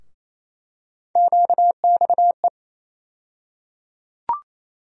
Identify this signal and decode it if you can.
Morse code, heard as a keyed tone, about 27 wpm, 700 Hz. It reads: QXE
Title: USA Call Sign Postfixes - 3 Letters 28wpm